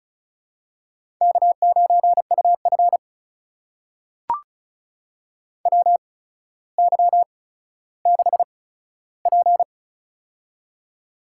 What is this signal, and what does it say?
Morse code, heard as a keyed tone, about 36 wpm, spaced as for 15 wpm, 700 Hz. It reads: K9UF WY6P
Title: USA Call Signs - 4 Characters - 8x Character Spacing - 35wpm